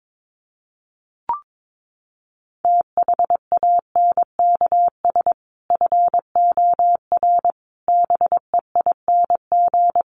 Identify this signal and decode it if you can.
Morse code, heard as a keyed tone, about 22 wpm, 700 Hz. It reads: THANKS FOR BEING
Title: Sets of 5 Words 22wpm